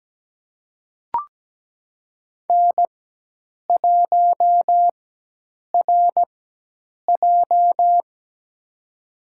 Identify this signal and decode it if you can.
Morse code, heard as a keyed tone, about 17 wpm, spaced as for 8 wpm, 700 Hz. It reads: N1RJ
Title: USA Call Signs - 4 Characters - 4x Character Spacing - 17wpm